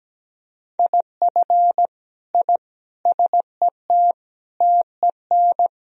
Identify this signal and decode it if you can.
Morse code, heard as a keyed tone, about 17 wpm, 700 Hz. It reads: IF I SET TEN
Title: Sentences from Top 1000 Words - Review Part 2 - 17wpm